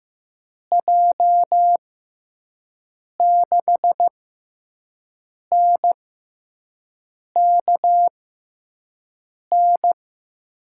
Morse code, heard as a keyed tone, about 15 wpm, spaced as for 5 wpm, 700 Hz. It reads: J6NKN
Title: USA Call Signs - 6x Character Spacing - 15wpm